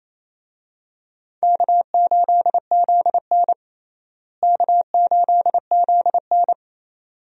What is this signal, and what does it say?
Morse code, heard as a keyed tone, about 28 wpm, 700 Hz. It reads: K8ZN K8ZN
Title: USA Call Signs - 4 Characters - Repeated 2x - 28wpm